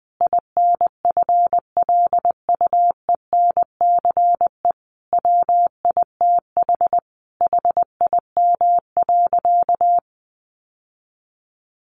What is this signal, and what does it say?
Morse code, heard as a keyed tone, about 20 wpm, 700 Hz. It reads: INFLUENCE WITH HIM.
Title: Sentences from Top 1100 Words 20wpm